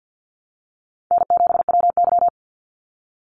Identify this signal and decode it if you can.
Morse code, heard as a keyed tone, about 48 wpm, 700 Hz. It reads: N7WY
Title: Call Signs Repeated 2x - 50wpm